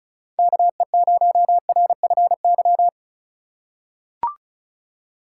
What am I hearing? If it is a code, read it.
Morse code, heard as a keyed tone, about 36 wpm, 700 Hz. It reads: KE0RFY
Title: USA Call Signs - 6 Characters - 3x Character Spacing - 35wpm